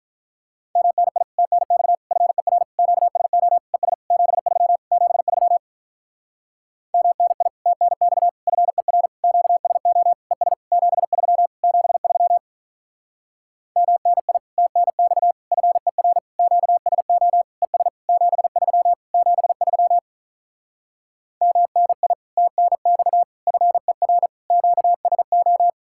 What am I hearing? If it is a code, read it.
Morse code, heard as a keyed tone, about 46 wpm, 700 Hz. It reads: MNI TNX FER QSO ES 73 73 MNI TNX FER QSO ES 73 73 MNI TNX FER QSO ES 73 73 MNI TNX FER QSO